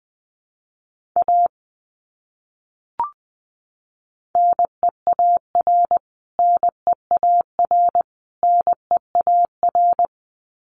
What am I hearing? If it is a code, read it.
Morse code, heard as a keyed tone, about 20 wpm, 700 Hz. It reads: A NEAR NEAR NEAR
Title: Top 200 Words Repeated 20wpm